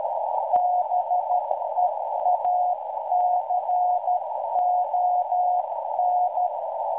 Morse code, heard as a keyed tone, about 12 wpm, 720 Hz, with noise. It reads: BEACON